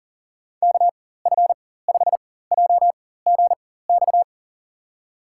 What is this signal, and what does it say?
Morse code, heard as a keyed tone, about 40 wpm, spaced as for 19 wpm, 700 Hz. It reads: KF5JGX
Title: USA Call Signs - 6 Characters - 4x Character Spacing - 40wpm